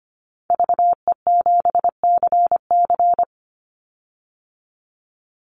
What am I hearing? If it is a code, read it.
Morse code, heard as a keyed tone, about 25 wpm, 700 Hz. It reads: VE7CC
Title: Call Signs Repeated 2x - 25wpm